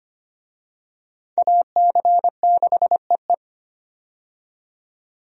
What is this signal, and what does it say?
Morse code, heard as a keyed tone, about 25 wpm, 700 Hz. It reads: AC6EE
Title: USA Call Signs - 5 Characters - 25wpm